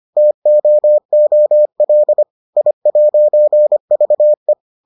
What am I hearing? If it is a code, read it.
Morse code, heard as a keyed tone, about 25 wpm, 600 Hz. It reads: TOOL I'VE